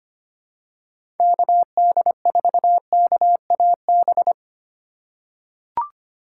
Morse code, heard as a keyed tone, about 25 wpm, 700 Hz. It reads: KD4KAB
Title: USA Call Signs - 6 Characters - 3x Character Spacing - 25wpm